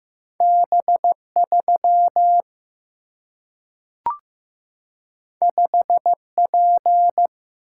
Morse code, heard as a keyed tone, about 15 wpm, 700 Hz. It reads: B3 5P
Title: Single-Letter Single-Number 15wpm